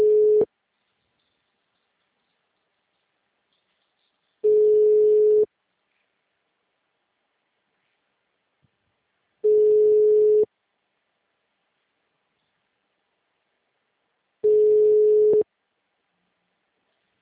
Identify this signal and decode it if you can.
Ringback tone (Europe, 425 Hz).
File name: out- -2016- - - .14